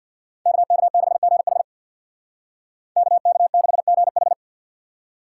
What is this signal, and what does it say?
Morse code, heard as a keyed tone, about 64 wpm, 700 Hz. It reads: KK6GH KK6GH